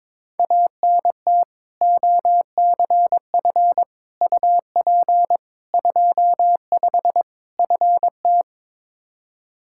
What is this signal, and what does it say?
Morse code, heard as a keyed tone, about 22 wpm, 700 Hz. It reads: ANT OCF UP 25 FT
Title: Practice QSO - Exchange 2 - Info - 22wpm